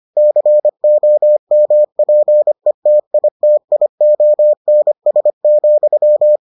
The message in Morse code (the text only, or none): COMPETITIONS,